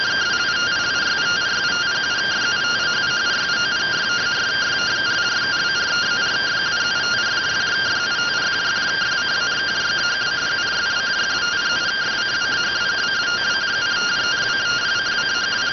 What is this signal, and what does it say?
Two-tone teleprinter signal, about 170 Hz shift.